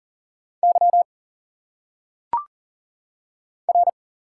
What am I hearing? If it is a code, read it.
Morse code, heard as a keyed tone, about 40 wpm, 700 Hz. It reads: Y R
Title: Single Letters - Phonetic Pronunciation 40wpm